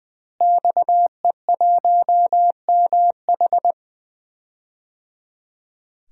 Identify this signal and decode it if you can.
Morse code, heard as a keyed tone, about 21 wpm, 700 Hz. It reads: XE1MH
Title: DXCC-100 Call Signs - 5 Characters - Repeated 3x - 20wpm